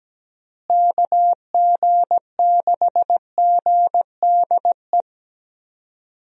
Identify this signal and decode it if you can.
Morse code, heard as a keyed tone, about 17 wpm, 700 Hz. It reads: KG6GDE
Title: USA Call Signs - 6 Characters - Repeated 2x - 17wpm